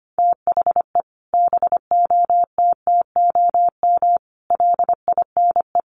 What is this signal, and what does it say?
Morse code, heard as a keyed tone, about 25 wpm, 700 Hz. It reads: THE BOTTOM LINE
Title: Sets of 5 Words Encore 25wpm